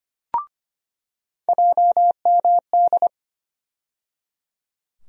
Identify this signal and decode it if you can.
Morse code, heard as a keyed tone, about 25 wpm, 700 Hz. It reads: JMD
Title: USA Call Sign Postfixes - 3 Letters 25wpm